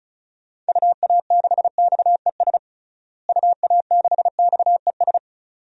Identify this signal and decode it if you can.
Morse code, heard as a keyed tone, about 36 wpm, 700 Hz. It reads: UA6XES UA6XES
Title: DXCC-100 Call Signs - 6 Characters - Repeated 2x - 35wpm